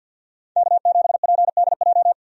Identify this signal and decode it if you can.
Morse code, heard as a keyed tone, about 51 wpm, 700 Hz. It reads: K7PDJ
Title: USA Call Signs - 5 Characters - 50wpm